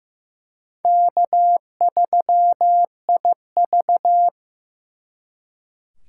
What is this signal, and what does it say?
Morse code, heard as a keyed tone, about 18 wpm, 700 Hz. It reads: K3IV
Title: Speed Racing - USA 1x2 Call Signs 15wpm